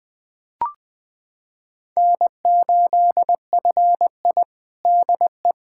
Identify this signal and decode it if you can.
Morse code, heard as a keyed tone, about 20 wpm, 700 Hz. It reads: N8FI DE
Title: USA Callsign DE Callsign - 4 Characters - 20wpm